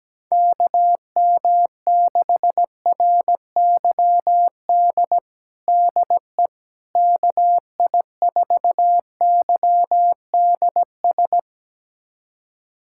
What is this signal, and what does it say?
Morse code, heard as a keyed tone, about 17 wpm, 700 Hz. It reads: KM6RYD DE KI4YDS
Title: USA Callsign DE Callsign - 6 Characters - 17wpm